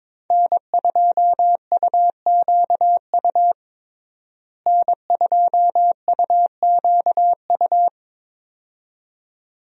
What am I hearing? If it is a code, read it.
Morse code, heard as a keyed tone, about 22 wpm, 700 Hz. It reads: N2UQU N2UQU
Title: USA Call Signs - 5 Characters - Repeated 2x - 22wpm